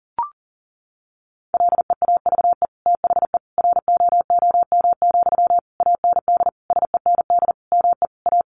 Morse code, heard as a keyed tone, about 40 wpm, 700 Hz. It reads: LEAVE THE ROOM, AND SEND ME A
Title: Sentences from Top 1500 Words 40wpm